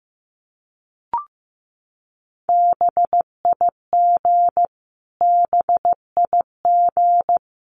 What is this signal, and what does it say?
Morse code, heard as a keyed tone, about 15 wpm, 700 Hz. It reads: BIG BIG
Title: Top 500 Words Repeated 15wpm